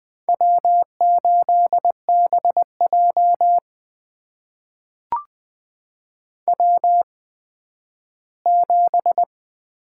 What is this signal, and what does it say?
Morse code, heard as a keyed tone, about 20 wpm, spaced as for 12 wpm, 700 Hz. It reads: W8BJ W7
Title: USA Call Signs - 4 Characters - 8x Character Spacing - 20wpm